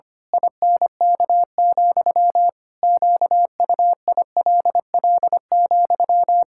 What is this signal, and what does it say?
Morse code, heard as a keyed tone, about 25 wpm, 700 Hz. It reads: INK, QUILL,
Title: Pangram Challenge v2 - 25wpm